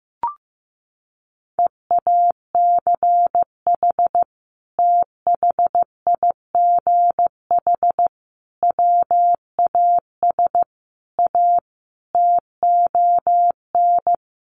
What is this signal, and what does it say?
Morse code, heard as a keyed tone, about 15 wpm, 700 Hz. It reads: EACH THIGH WAS A TON
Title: A Sound Of Thunder 15wpm